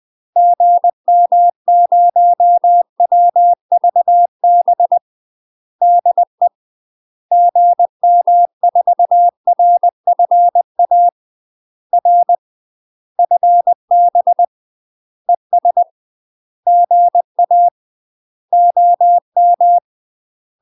Morse code, heard as a keyed tone, about 20 wpm, 700 Hz. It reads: GM0WVB DE GM4RFA R FB ES GA OM
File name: UK MASTER QSOs TRAINING / 20WPM / UK MASTER QSOs 0022